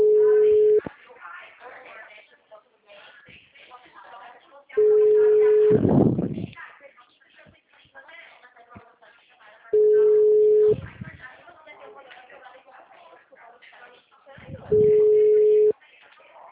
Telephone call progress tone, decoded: ringback tone

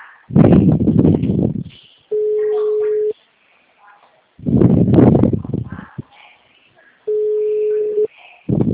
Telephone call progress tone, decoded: ringback tone

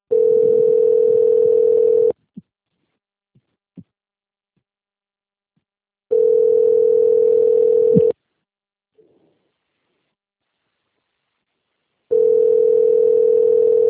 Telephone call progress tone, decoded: ringback tone

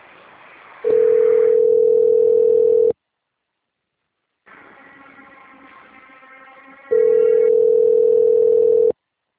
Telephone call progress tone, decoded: ringback tone

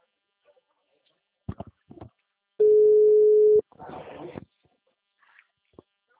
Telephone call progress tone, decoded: ringback tone

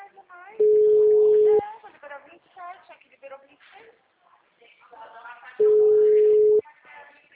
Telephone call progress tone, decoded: ringback tone

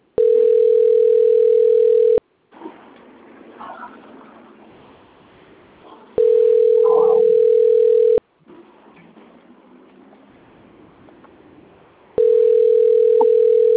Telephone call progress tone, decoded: ringback tone

